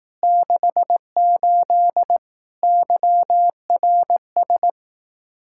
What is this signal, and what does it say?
Morse code, heard as a keyed tone, about 18 wpm, 700 Hz. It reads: 68 YRS
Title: Speed-Racing Common QSO Phrases Encore - 15wpm